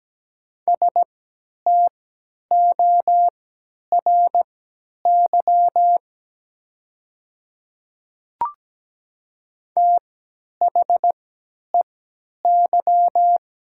Morse code, heard as a keyed tone, about 17 wpm, spaced as for 10 wpm, 700 Hz. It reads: STORY THEY
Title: 3x Character Spacing - Top 300 Words - 17wpm